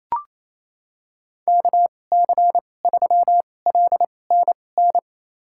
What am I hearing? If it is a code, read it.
Morse code, heard as a keyed tone, about 28 wpm, spaced as for 20 wpm, 700 Hz. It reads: KC3LNN
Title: USA Call Signs - 2x Character Spacing - 28wpm